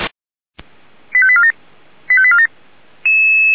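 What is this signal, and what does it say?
Five-tone selective call (ZVEI), heard as quick runs of five tones, tones of about 70 ms. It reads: 85847 85847
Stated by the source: ZVEI five-tone selective call 89386 received on SDR.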